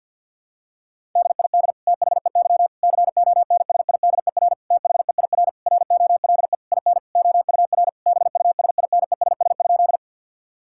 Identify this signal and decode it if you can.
Morse code, heard as a keyed tone, about 50 wpm, 700 Hz. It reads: DID THEY CONSIDER THEIR ROLE IN OUR BUSINESS?
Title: Sentences from Top 1000 Words - Review Part 6 - 50wpm